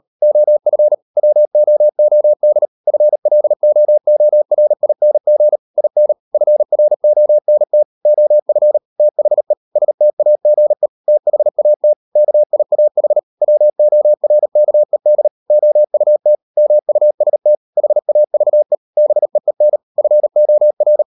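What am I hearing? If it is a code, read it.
Morse code, heard as a keyed tone, about 38 wpm, 600 Hz. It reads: OF WOOD FLOORING IN FRONT OF THE STAGE THAT KIAH WORKED OUT MUST HAVE BEEN FOR
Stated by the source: Morse code practice at 38 WPM, sending Ken Follett's novel Never in English, from CW med Gnister.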